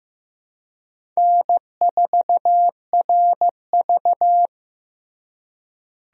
Morse code, heard as a keyed tone, about 15 wpm, 700 Hz. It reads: N4RV
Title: USA Call Signs - 4 Characters - Repeated 3x - 15wpm